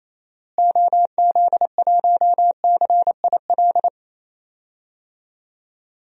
Morse code, heard as a keyed tone, about 28 wpm, 700 Hz. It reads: OZ1CIL
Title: DXCC-100 Call Signs - 6 Characters - Repeated 2x - 28wpm